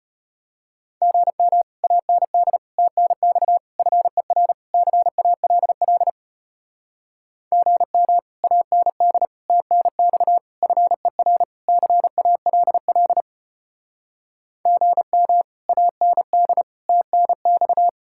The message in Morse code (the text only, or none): GM AND TNX FER CALL GM AND TNX FER CALL GM AND TNX